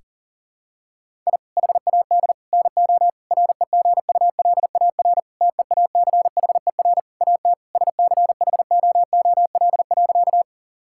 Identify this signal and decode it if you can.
Morse code, heard as a keyed tone, about 40 wpm, 700 Hz. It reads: I HAD NO REGULAR TEACHER AT SCHOOL.
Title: Sentences from Top 1000 Words - Review Part 6 - 40wpm